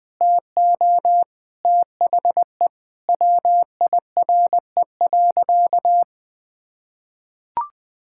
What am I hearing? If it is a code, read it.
Morse code, heard as a keyed tone, about 20 wpm, 700 Hz. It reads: TO THE WIRE.
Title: Sentences with 3-Word Phrases - 20wpm